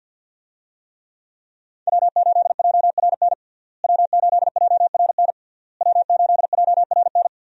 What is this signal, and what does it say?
Morse code, heard as a keyed tone, about 53 wpm, 700 Hz. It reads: W8JRN W8JRN W8JRN
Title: USA Call Signs - 5 Characters - Repeated 3x - 50wpm